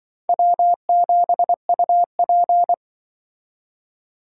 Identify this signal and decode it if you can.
Morse code, heard as a keyed tone, about 25 wpm, 700 Hz. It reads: W7UP
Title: Speed Racing - USA 1x2 Call Signs 20wpm